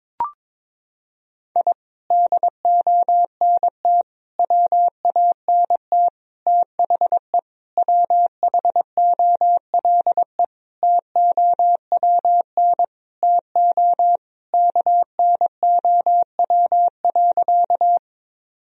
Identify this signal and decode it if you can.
Morse code, heard as a keyed tone, about 22 wpm, 700 Hz. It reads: I DONT WANT THE WHOLE TOWN TO KNOW.
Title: Sentences from Top 1000 Words - Review Part 2 - 22wpm